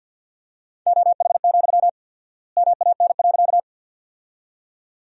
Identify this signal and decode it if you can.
Morse code, heard as a keyed tone, about 50 wpm, 700 Hz. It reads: OH, MAN.
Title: 4x Word-Spacing - Sentences from Top 500 Words - 50wpm